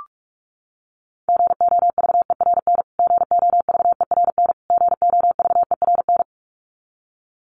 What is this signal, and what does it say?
Morse code, heard as a keyed tone, about 44 wpm, 700 Hz. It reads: GOVERN GOVERN GOVERN
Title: Top 500 Words Repeated 45wpm